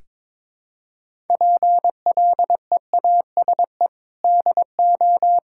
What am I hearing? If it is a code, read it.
Morse code, heard as a keyed tone, about 22 wpm, 700 Hz. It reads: PLEASE DO
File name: practice / 22wpm / Sentences with 4-Word Phrases - 22wpm